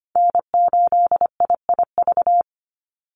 Morse code, heard as a keyed tone, about 25 wpm, 700 Hz. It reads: N8IIV
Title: USA Call Signs 25wpm